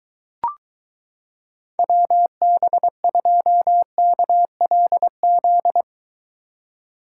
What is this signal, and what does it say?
Morse code, heard as a keyed tone, about 23 wpm, 700 Hz. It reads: WB2KLZ